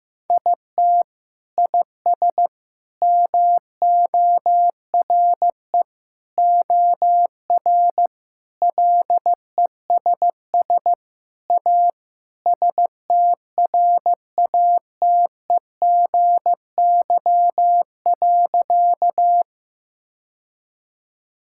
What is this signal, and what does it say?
Morse code, heard as a keyed tone, about 15 wpm, 700 Hz. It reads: IT IS MORE OR LESS A STRATEGY.
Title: Sentences from Top 1000 Words - Review Part 5 - 15wpm